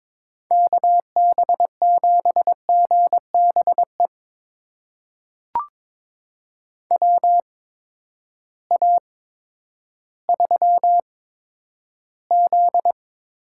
Morse code, heard as a keyed tone, about 21 wpm, spaced as for 11 wpm, 700 Hz. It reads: KB7GBE WA3Z
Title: USA Call Signs - 6 Characters - 8x Character Spacing - 22wpm